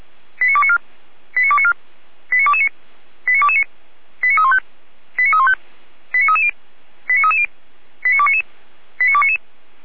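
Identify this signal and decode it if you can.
Five-tone selective call (ZVEI), heard as quick runs of five tones, tones of about 70 ms. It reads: 89294 89294 89229 89229 89315 89315 89330 89330 89299 89299
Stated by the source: ZVEI five-tone call 89277 received by SDR, from audio.f85360.feuerwehr-orlamuende.org fire brigade (Feuerwehr Orlamünde) archive.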